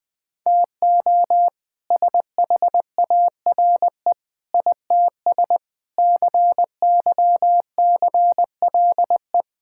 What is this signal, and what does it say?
Morse code, heard as a keyed tone, about 20 wpm, 700 Hz. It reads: TO SHARE ITS CYCLE
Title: All-Age Appropriate Jokes - 20wpm